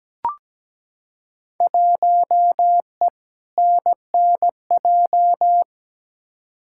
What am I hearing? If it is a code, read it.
Morse code, heard as a keyed tone, about 17 wpm, 700 Hz. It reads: 1E NNJ
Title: Field Day Exchanges - 17wpm